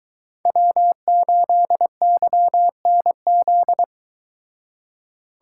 Morse code, heard as a keyed tone, about 22 wpm, 700 Hz. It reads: W8YNZ